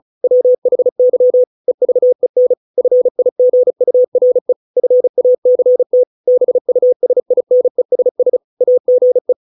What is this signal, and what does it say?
Morse code, heard as a keyed tone, about 35 wpm, 500 Hz. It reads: WHY EVEN FIGURE FACT BUSINESS AGE